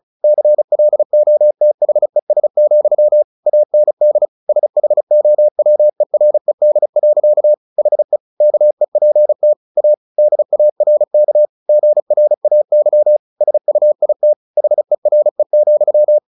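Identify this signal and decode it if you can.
Morse code, heard as a keyed tone, about 34 wpm, 600 Hz. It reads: CLOTHES, AND SHOWERED. HE KEPT A DARK GRAY SUIT HERE,
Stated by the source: Morse practice at 35 wpm from CW med Gnister, sending Ken Follett's novel Never in English.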